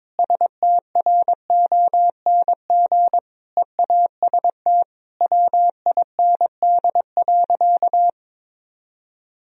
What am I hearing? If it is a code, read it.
Morse code, heard as a keyed tone, about 22 wpm, 700 Hz. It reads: STRONG EAST WIND.
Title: Sentences from Top 1000 Words - Review Part 3 - 22wpm